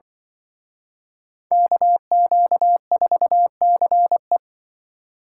Morse code, heard as a keyed tone, about 25 wpm, 700 Hz. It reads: KQ4CE
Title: Speed Racing - USA 2x2 Call Signs 20wpm